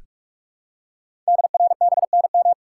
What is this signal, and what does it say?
Morse code, heard as a keyed tone, about 44 wpm, 700 Hz. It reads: DRBNM
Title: Code Groups - 5 Letters - 45wpm